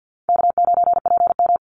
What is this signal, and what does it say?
Morse code, heard as a keyed tone, about 50 wpm, 700 Hz. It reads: K9PM
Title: NA SOTA Chaser Callsigns 50wpm